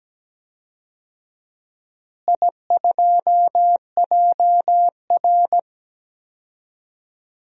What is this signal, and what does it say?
Morse code, heard as a keyed tone, about 17 wpm, 700 Hz. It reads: I2JR
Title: DXCC-100 Call Signs - 4 Characters - Repeated 2x - 17wpm